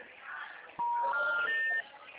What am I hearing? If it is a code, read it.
Special information tone, with noise.